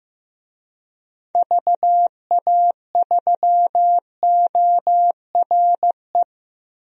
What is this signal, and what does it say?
Morse code, heard as a keyed tone, about 15 wpm, 700 Hz. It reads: VA3ORE
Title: DXCC-100 Call Signs - 6 Characters - Repeated 2x - 15wpm